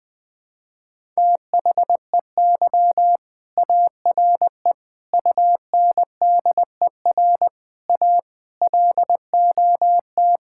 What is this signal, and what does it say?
Morse code, heard as a keyed tone, about 21 wpm, 700 Hz. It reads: THEY ARE UNDER A LOT
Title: Sentences from Top 1000 Words - Review Part 6 - 20wpm